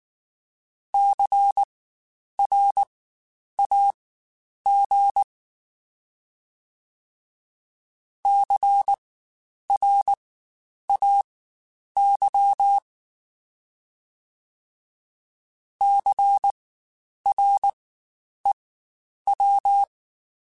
Morse code, heard as a keyed tone, about 19 wpm, spaced as for 9 wpm, 800 Hz. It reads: CRAG CRAY CREW
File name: Spaced Alphas / Spaced Alphas mp3s / Cs sp @18